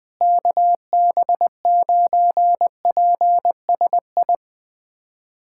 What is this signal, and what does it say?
Morse code, heard as a keyed tone, about 20 wpm, 700 Hz. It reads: KB9PSI
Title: USA Callsign DE Callsign - 6 Characters - 20wpm